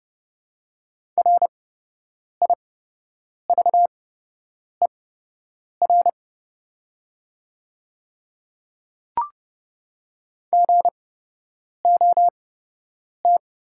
Morse code, heard as a keyed tone, about 30 wpm, spaced as for 8 wpm, 690 Hz. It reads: RIVER GOT
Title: 8x Character Spacing - Top 300 Words - 30wpm